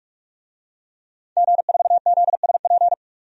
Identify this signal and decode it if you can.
Morse code, heard as a keyed tone, about 43 wpm, 700 Hz. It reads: G4ZSP
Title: DXCC-100 Call Signs - 5 Characters - 45wpm